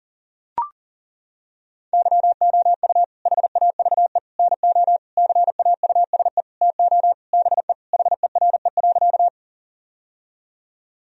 Morse code, heard as a keyed tone, about 40 wpm, 700 Hz. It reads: YOU HAVE NO CAUSE TO BE HERE.
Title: Sentences from Top 1000 Words - Review Part 1 - 40wpm